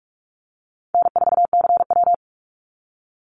Morse code, heard as a keyed tone, about 44 wpm, 700 Hz. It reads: N4CW
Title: Call Signs Repeated 2x - 45wpm